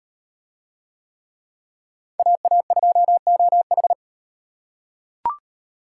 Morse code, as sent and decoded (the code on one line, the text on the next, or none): .- .- ..--- --- ....
AA2OH